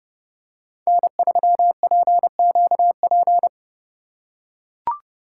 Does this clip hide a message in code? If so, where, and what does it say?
Yes: Morse code from 0.9 to 3.5 s, heard as a keyed tone, about 30 wpm, 700 Hz. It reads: N3PQP